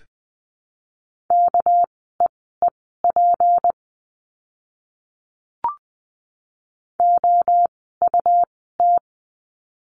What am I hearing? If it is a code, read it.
Morse code, heard as a keyed tone, about 20 wpm, spaced as for 14 wpm, 700 Hz. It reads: KEEP OUT
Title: 244 Lesson 39f - Letter z - Words 20wpm